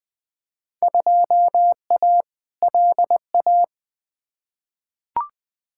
Morse code, heard as a keyed tone, about 20 wpm, 700 Hz. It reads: 2A LA
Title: Field Day Exchanges - 20wpm